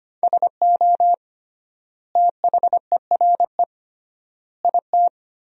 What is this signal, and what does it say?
Morse code, heard as a keyed tone, about 25 wpm, 700 Hz. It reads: SO THERE IT
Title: 3x Word-Spacing - Sentences from Top 100 Words - 25wpm